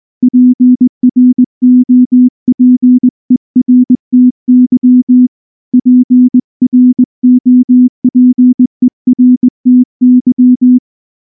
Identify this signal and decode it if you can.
Morse code, heard as a keyed tone, about 19 wpm, 250 Hz. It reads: PROPERTY PROPERTY
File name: morse / morse-training-20WPM-10